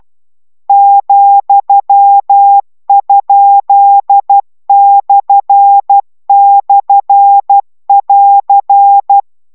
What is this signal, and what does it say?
Morse code, heard as a keyed tone, about 12 wpm, 800 Hz. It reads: ,?//<AR>